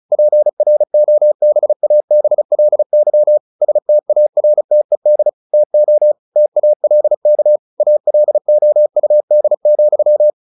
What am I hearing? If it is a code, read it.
Morse code, heard as a keyed tone, about 34 wpm, 600 Hz. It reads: PROBABLY STARTED TO TALK ALOUD,
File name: the old man and the sea / the old man and the sea 1 35wpm